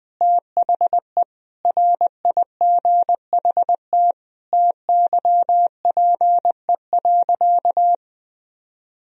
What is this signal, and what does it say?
Morse code, heard as a keyed tone, about 20 wpm, 700 Hz. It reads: THE RIGHT TYPE.
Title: Sentences from Top 1000 Words - Review Part 3 - 20wpm